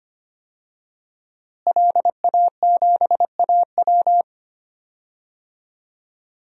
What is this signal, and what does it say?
Morse code, heard as a keyed tone, about 25 wpm, 700 Hz. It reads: LA7AW
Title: DXCC-100 Call Signs - 5 Characters - Repeated 3x - 25wpm